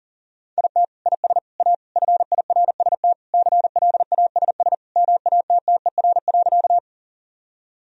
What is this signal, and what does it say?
Morse code, heard as a keyed tone, about 40 wpm, 700 Hz. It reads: IT IS A FIRST CLASS MATTER.
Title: Sentences from Top 1000 Words - Review Part 2 - 40wpm